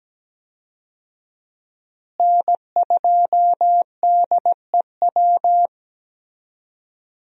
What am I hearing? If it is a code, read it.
Morse code, heard as a keyed tone, about 17 wpm, 700 Hz. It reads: N2DEW
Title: USA Call Signs - 5 Characters - Repeated 3x - 17wpm